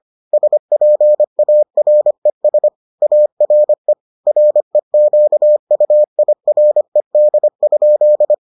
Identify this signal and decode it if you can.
Morse code, heard as a keyed tone, about 25 wpm, 600 Hz. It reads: SPARES ARE REQUIRED?